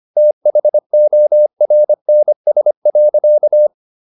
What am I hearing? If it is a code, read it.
Morse code, heard as a keyed tone, about 25 wpm, 600 Hz. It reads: THORNS.